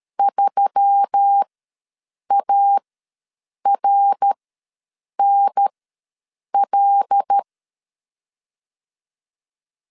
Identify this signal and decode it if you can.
Morse code, heard as a keyed tone, about 13 wpm, spaced as for 7 wpm, 800 Hz. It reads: 3ARNL
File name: LEZIONI / 5 LBVC / 5 L ws 12wpm 2958